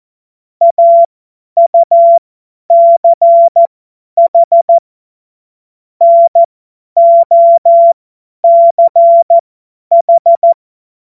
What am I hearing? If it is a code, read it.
Morse code, heard as a keyed tone, about 14 wpm, spaced as for 10 wpm, 680 Hz. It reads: AUCH NOCH
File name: Morsekurs / Witze / Witz 04. Juli 2023 50 F70